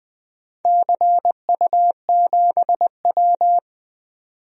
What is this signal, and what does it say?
Morse code, heard as a keyed tone, about 21 wpm, 700 Hz. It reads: CU7W
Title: DXCC-100 Call Signs - 4 Characters - 20wpm